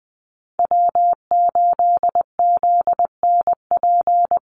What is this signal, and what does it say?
Morse code, heard as a keyed tone, about 20 wpm, 700 Hz. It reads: W8ZNP
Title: USA Call Signs Repeated 3x 20wpm